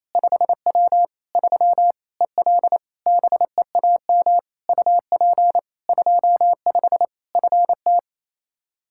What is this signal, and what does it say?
Morse code, heard as a keyed tone, about 27 wpm, 700 Hz. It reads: 5W 3 EL BEAM UP 25 FT
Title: Practice QSO - Exchange 2 - Info - 28wpm